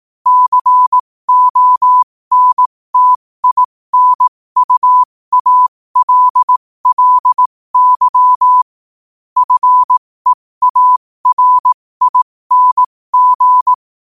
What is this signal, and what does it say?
Morse code, heard as a keyed tone, about 18 wpm, spaced as for 15 wpm, 1000 Hz. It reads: CONTINUALLY FEARING
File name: qotdfiles / 15wpm